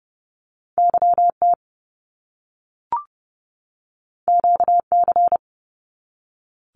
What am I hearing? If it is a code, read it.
Morse code, heard as a keyed tone, about 30 wpm, 700 Hz. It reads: YT QC
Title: Canadian Provinces and Territories 30wpm